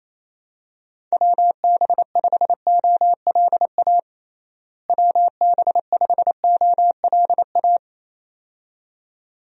Morse code, heard as a keyed tone, about 28 wpm, 700 Hz. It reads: WB5OLA WB5OLA
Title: USA Call Signs - 6 Characters - Repeated 2x - 28wpm